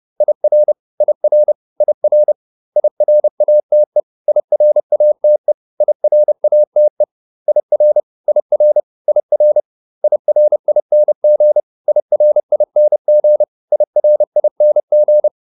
Morse code, heard as a keyed tone, about 30 wpm, 600 Hz. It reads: IR IR IR IRATE IRATE IRATE IR IR IR IRING IRING IRING